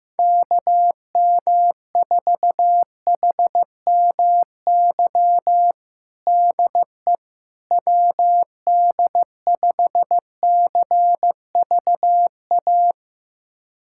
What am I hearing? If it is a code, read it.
Morse code, heard as a keyed tone, about 15 wpm, 700 Hz. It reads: KM4HMY DE WD5CVA
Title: USA Callsign DE Callsign - 6 Characters - 15wpm